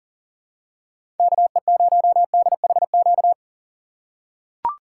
Morse code, heard as a keyed tone, about 40 wpm, 700 Hz. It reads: KE0DHQ
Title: USA Call Signs - 6 Characters - 4x Character Spacing - 40wpm